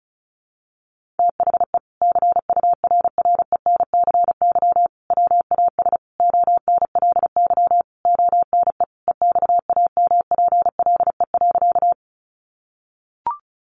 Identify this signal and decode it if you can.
Morse code, heard as a keyed tone, about 34 wpm, 700 Hz. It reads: THE CURRENCY WAS ONLY ONE EXAMPLE.
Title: Sentences from Top 1500 Words 35wpm